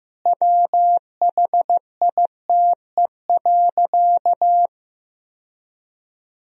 Morse code, heard as a keyed tone, about 16 wpm, 700 Hz. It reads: WHITE.
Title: Sentences from Top 1000 Words - Review Part 2 - 15wpm